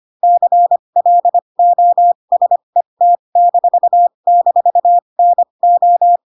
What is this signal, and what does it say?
Morse code, heard as a keyed tone, about 25 wpm, spaced as for 22 wpm, 700 Hz. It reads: CLOSET--NO